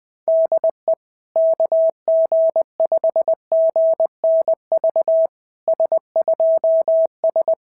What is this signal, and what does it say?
Morse code, heard as a keyed tone, about 20 wpm, 660 Hz. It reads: DE KG5GNV S2S
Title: SOTA QSO Practice 20wpm